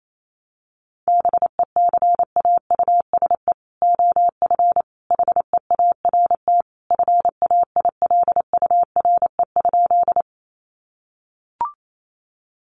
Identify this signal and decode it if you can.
Morse code, heard as a keyed tone, about 28 wpm, 700 Hz. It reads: BECAUSE OF HEART FAILURE?
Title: Sentences from Top 1200 Words 28wpm